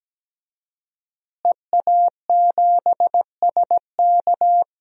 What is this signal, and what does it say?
Morse code, heard as a keyed tone, about 17 wpm, 700 Hz. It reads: EA7SK
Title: DXCC-100 Call Signs - 5 Characters - Repeated 3x - 17wpm